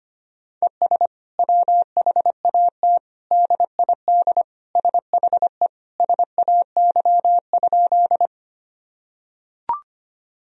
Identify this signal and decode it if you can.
Morse code, heard as a keyed tone, about 25 wpm, 700 Hz. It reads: ES WHAT DID SHE SAY?
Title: Sentences from Top 200 Words and Abbreviations 25wpm